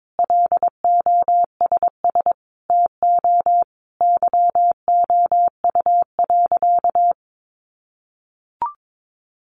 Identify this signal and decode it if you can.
Morse code, heard as a keyed tone, about 22 wpm, 700 Hz. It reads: LOSS TO YOU.
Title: Sentences from Top 1400 Words 22wpm